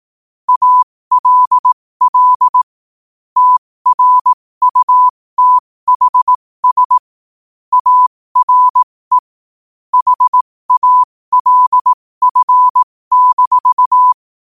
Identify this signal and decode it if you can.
Morse code, heard as a keyed tone, about 18 wpm, spaced as for 15 wpm, 1000 Hz. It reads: ALL TRUTHS ARE HALF-
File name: qotdfiles / 15wpm